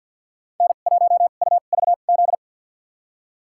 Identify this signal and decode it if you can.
Morse code, heard as a keyed tone, about 52 wpm, spaced as for 37 wpm, 700 Hz. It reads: N1UVZ
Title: USA Call Signs - 5 Characters - 2x Character Spacing - 50wpm